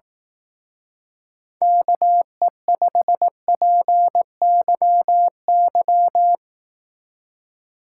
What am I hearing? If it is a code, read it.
Morse code, heard as a keyed tone, about 18 wpm, 700 Hz. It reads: KE5PYY
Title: Speed Racing - USA 2x3 Call Signs 15wpm